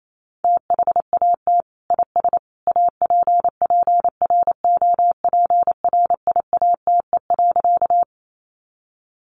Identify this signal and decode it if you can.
Morse code, heard as a keyed tone, about 28 wpm, 700 Hz. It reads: THAT IS APPROPRIATE.
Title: Sentences from Top 900 Words 28wpm